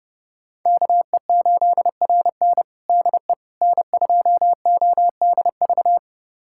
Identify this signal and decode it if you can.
Morse code, heard as a keyed tone, about 30 wpm, 700 Hz. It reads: KE8RN DE N2ODV
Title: USA Callsign DE Callsign - 5 Characters - 30wpm